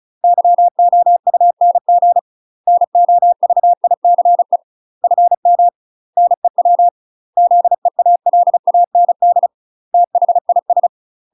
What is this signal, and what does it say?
Morse code, heard as a keyed tone, about 34 wpm, 700 Hz. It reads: YOUNG NOVICE FM NEW ZEALAND THIS
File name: UK MASTER QSOs TRAINING / 35WPM / UK MASTER QSOs 0021